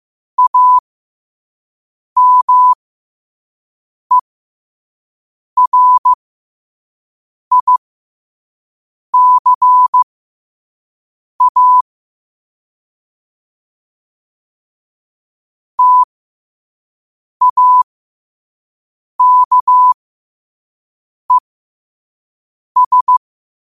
Morse code, heard as a keyed tone, about 15 wpm, spaced as for 5 wpm, 1000 Hz. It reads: AMERICA TAKES
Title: 5 WPM morse code quotes for Fri, 01 Mar 2024